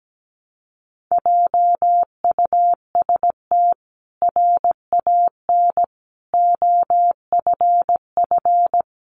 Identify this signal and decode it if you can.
Morse code, heard as a keyed tone, about 17 wpm, 700 Hz. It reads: JUST RAN OFF
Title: A Sound Of Thunder 17wpm